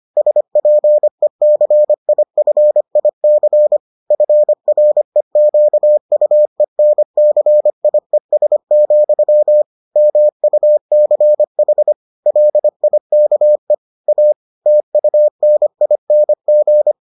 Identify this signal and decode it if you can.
Morse code, heard as a keyed tone, about 25 wpm, 600 Hz. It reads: SPECIFIC FREQUENCIES, MUCH LIKE A TUNING